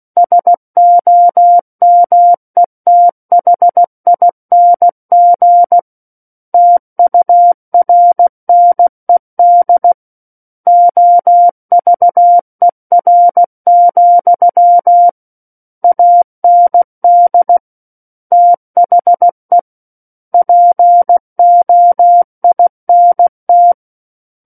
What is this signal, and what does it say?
Morse code, heard as a keyed tone, about 16 wpm, 700 Hz. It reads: SOMETHING TURNED OVER, AND THE POINT
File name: Book One / 07-Chapter 7 - 16 WPM